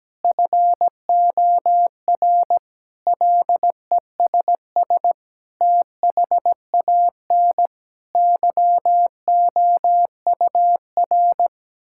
Morse code, heard as a keyed tone, about 17 wpm, 700 Hz. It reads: FOR LESS THAN YOUR
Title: Sentences from Top 1000 Words - Review Part 5 - 17wpm